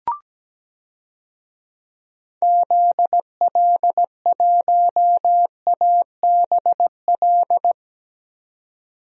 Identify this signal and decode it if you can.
Morse code, heard as a keyed tone, about 17 wpm, 700 Hz. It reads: ZL1ABL
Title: DXCC-100 Call Signs - 6 Characters - 17wpm